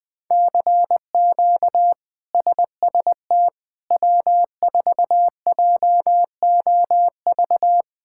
Morse code, heard as a keyed tone, about 21 wpm, 700 Hz. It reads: CQ SST W4JOV
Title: K1USN SST QSO Practice - 20wpm